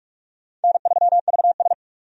Morse code, heard as a keyed tone, about 42 wpm, 700 Hz. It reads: N3VS
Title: USA Call Signs - 4 Characters - 45wpm